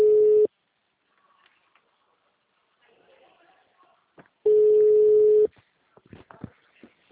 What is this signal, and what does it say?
Ringback tone (Europe, 425 Hz).
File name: out- -2019- - - .7200